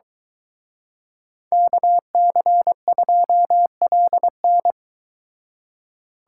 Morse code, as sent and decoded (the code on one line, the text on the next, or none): -.- -.-. ..--- .-.. -.
KC2LN